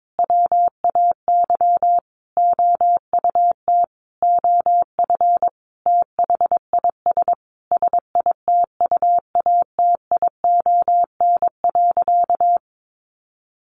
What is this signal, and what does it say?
Morse code, heard as a keyed tone, about 22 wpm, 700 Hz. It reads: WAY OUT OF THIS SITUATION.
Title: Sentences from Top 1200 Words 22wpm